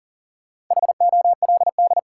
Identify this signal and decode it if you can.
Morse code, heard as a keyed tone, about 40 wpm, 700 Hz. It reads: HOLD
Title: Sentences from Top 400 Words - New Word Drill - 40wpm